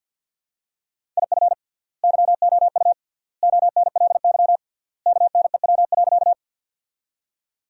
Morse code, heard as a keyed tone, about 50 wpm, 700 Hz. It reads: IF YOU ONLY KNEW.
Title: 3x Word-Spacing - Sentences from Top 400 Words - 50wpm